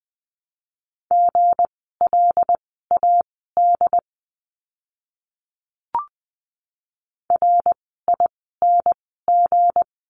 Morse code, heard as a keyed tone, about 20 wpm, spaced as for 15 wpm, 700 Hz. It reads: GLAD RING
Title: 160 Lesson 26f - Letter g - Words 20wpm